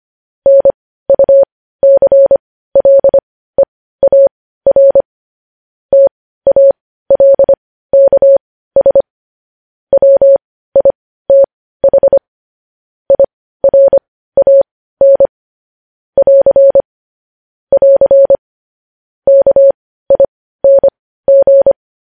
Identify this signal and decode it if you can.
Morse code, heard as a keyed tone, about 25 wpm, spaced as for 15 wpm, 550 Hz. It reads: NUCLEAR TALKS WITH IRAN <AR> <AR> KING
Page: news-headlines-in-morse-code-at-15-wpm